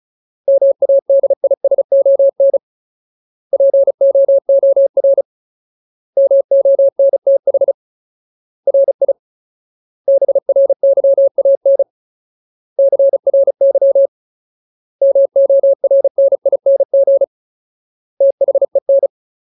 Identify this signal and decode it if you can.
Morse code, heard as a keyed tone, about 34 wpm, 550 Hz. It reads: MADISON POOR MONTH RI BRYAN CRY MORNING THEN
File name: Ultimate Word List / 35WPM Ultimate